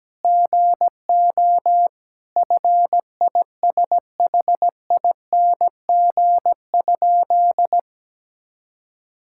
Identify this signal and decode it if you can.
Morse code, heard as a keyed tone, about 17 wpm, 700 Hz. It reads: GO FISHING?
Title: Sentences from Top 1000 Words - Review Part 5 - 17wpm